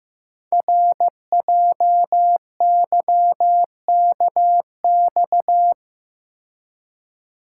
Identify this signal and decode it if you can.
Morse code, heard as a keyed tone, about 15 wpm, 700 Hz. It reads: RJYKX
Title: Code Groups - 5 Letters - 15wpm